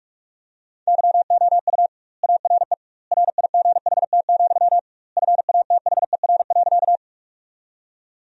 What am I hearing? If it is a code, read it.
Morse code, heard as a keyed tone, about 48 wpm, 700 Hz. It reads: YOU ARE RIGHT, FATHER.
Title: 2x Word-Spacing - Sentences from Top 300 Words - 45wpm